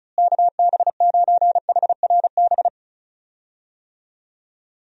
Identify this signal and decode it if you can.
Morse code, heard as a keyed tone, about 36 wpm, 700 Hz. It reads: KB9HRB
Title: USA Call Signs - 6 Characters - Repeated 2x - 35wpm